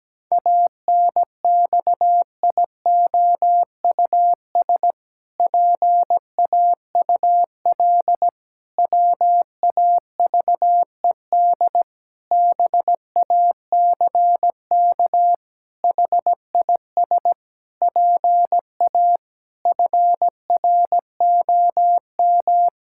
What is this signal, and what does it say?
Morse code, heard as a keyed tone, about 17 wpm, 700 Hz. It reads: ANXIOUS PAUL WAVED BACK HIS PA FROM
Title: Pangram Challenge v2 - 17wpm